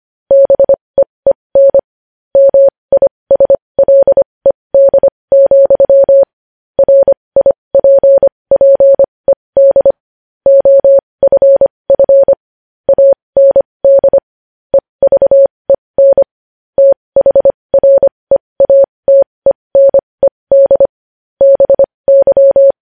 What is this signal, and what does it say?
Morse code, heard as a keyed tone, about 25 wpm, spaced as for 20 wpm, 550 Hz. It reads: BEEN MISLED, RIPPED OFF AND EVEN THREATENED BY
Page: News headlines in Morse code at 20 wpm